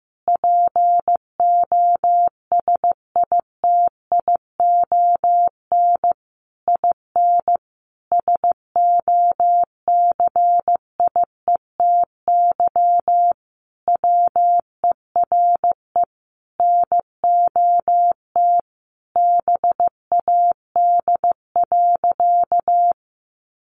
Morse code, heard as a keyed tone, about 15 wpm, 700 Hz. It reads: POSITION IN SOCIETY WERE NOT BAD.
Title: Sentences from Top 900 Words 15wpm